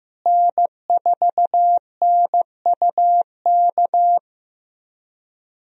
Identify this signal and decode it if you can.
Morse code, heard as a keyed tone, about 15 wpm, 700 Hz. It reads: N4NUK
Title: USA Call Signs - 5 Characters - Repeated 2x - 15wpm